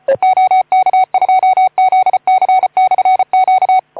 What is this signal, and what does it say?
Morse code, heard as a keyed tone, about 28 wpm, 750 Hz. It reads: OK2ZC/Q
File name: Archiv / CUC181 / OK2ZC CUC 181